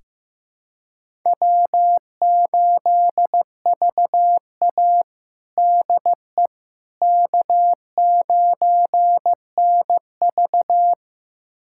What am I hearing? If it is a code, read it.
Morse code, heard as a keyed tone, about 15 wpm, 700 Hz. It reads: W8VA DE K9NV
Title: USA Callsign DE Callsign - 4 Characters - 15wpm